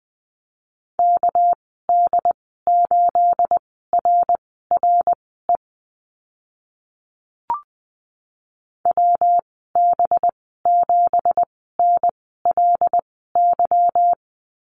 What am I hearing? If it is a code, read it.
Morse code, heard as a keyed tone, about 19 wpm, spaced as for 14 wpm, 700 Hz. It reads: KD8RRE WB7NLY
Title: 219 Lesson 35g - Number 0 - Call Signs 20wpm